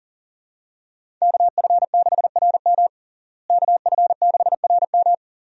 Morse code, heard as a keyed tone, about 40 wpm, 700 Hz. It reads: KF6RM KF6RM
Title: USA Call Signs - 5 Characters - Repeated 3x - 40wpm